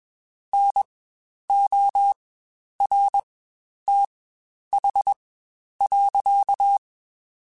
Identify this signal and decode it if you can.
Morse code, heard as a keyed tone, about 21 wpm, spaced as for 10 wpm, 800 Hz. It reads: NORTH.
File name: Spaced Characters / Spaced Characters-mp3s / Weather 4 24 sp @20